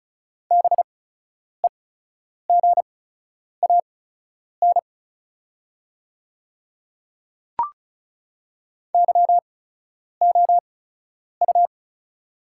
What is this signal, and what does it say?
Morse code, heard as a keyed tone, about 36 wpm, spaced as for 10 wpm, 690 Hz. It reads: BEGAN YOU
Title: 8x Character Spacing - Top 500 Words - 35wpm